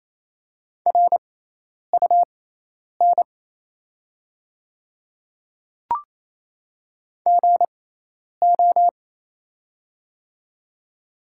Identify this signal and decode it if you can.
Morse code, heard as a keyed tone, about 27 wpm, spaced as for 10 wpm, 690 Hz. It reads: RUN GO
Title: 6x Character Spacing - Top 300 Words - 28wpm